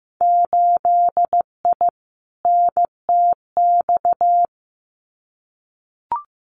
Morse code, heard as a keyed tone, about 15 wpm, 700 Hz. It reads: 8I NTX
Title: Winter Field Day Exchanges Encore 15wpm